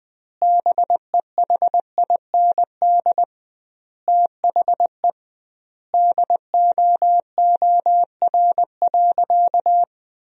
Morse code, heard as a keyed tone, about 19 wpm, 700 Hz. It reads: BEHIND THE DOOR.
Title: 2x Word-Spacing - Sentences from Top 500 Words - 20wpm